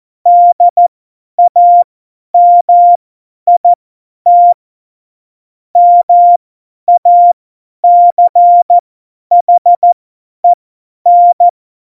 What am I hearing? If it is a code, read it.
Morse code, heard as a keyed tone, about 14 wpm, spaced as for 10 wpm, 700 Hz. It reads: DAMIT MACHEN
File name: Morsekurs / Witze / Witz 22. September 2020, 50 F70